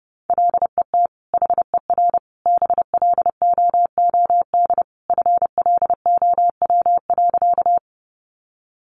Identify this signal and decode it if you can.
Morse code, heard as a keyed tone, about 30 wpm, 700 Hz. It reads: LET HER BLOOD FLOW.
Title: Sentences from Top 1200 Words 30wpm